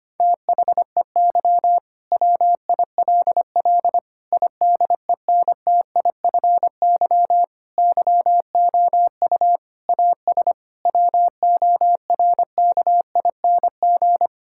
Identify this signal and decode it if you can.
Morse code, heard as a keyed tone, about 25 wpm, 700 Hz. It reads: THEY WILL IDENTIFY YOU AS WORKING
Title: Sentences from Top 1000 Words - Review Part 6 - 25wpm